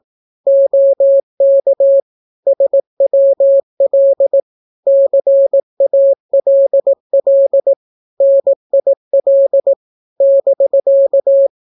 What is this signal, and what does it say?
Morse code, heard as a keyed tone, about 18 wpm, 550 Hz. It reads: OK SWL CALL NIL <BK>
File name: naukapodstaw / skroty / skroty